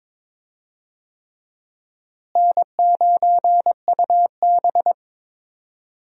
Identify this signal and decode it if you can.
Morse code, heard as a keyed tone, about 22 wpm, 700 Hz. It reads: N9UB
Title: USA Call Signs - 4 Characters - Repeated 3x - 22wpm